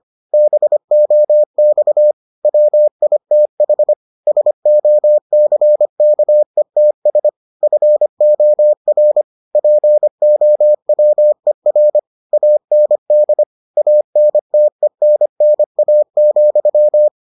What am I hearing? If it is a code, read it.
Morse code, heard as a keyed tone, about 25 wpm, 600 Hz. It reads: BOX WITH SOCKETS FOR POWER AND ANTENNA,